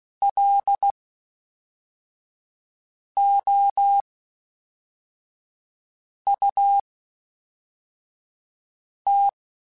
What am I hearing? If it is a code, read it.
Morse code, heard as a keyed tone, about 16 wpm, spaced as for 4 wpm, 800 Hz. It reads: LOUT
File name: Spaced / FL Runs sp mp3s / FL Runs sp @5